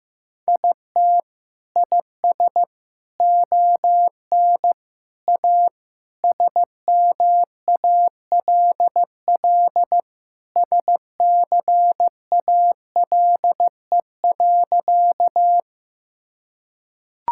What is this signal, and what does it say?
Morse code, heard as a keyed tone, about 15 wpm, 700 Hz. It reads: IT IS ON A SMALL SCALE.
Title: Sentences from Top 1000 Words - Review Part 5 - 15wpm